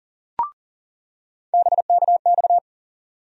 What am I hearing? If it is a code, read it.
Morse code, heard as a keyed tone, about 40 wpm, 700 Hz. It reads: BKX
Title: USA Call Sign Postfixes - 3 Letters 40wpm